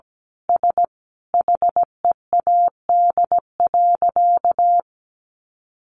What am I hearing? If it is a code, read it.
Morse code, heard as a keyed tone, about 17 wpm, 700 Hz. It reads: S HEAD.
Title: Easy Phrases 17wpm